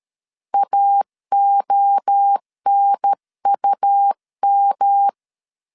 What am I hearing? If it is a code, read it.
Morse code, heard as a keyed tone, about 13 wpm, 800 Hz. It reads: AONUM
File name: CW / DET / 1 3 1500C S A 12wpm 2905